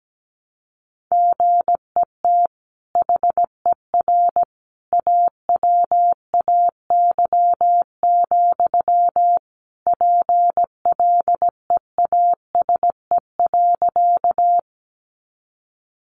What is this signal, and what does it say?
Morse code, heard as a keyed tone, about 17 wpm, 700 Hz. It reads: GET HER AWAY, PLEASE.
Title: Sentences from Top 1500 Words 17wpm